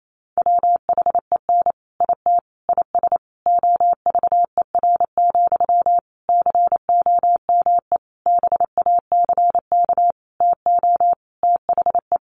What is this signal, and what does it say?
Morse code, heard as a keyed tone, about 28 wpm, 700 Hz. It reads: WHEN IT IS OVER, COME BACK TO THE
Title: Sentences from Top 1200 Words 28wpm